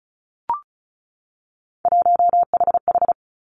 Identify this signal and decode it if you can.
Morse code, heard as a keyed tone, about 36 wpm, 700 Hz. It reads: 1HH
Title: Three Challenging Letter-Number Combinations 35wpm